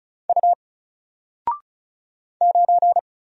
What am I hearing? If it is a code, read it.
Morse code, heard as a keyed tone, about 36 wpm, 700 Hz. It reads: U 9
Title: Single Letter-Number Alt 35wpm